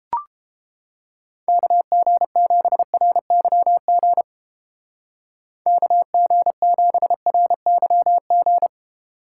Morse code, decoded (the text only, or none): KG7RYG KG7RYG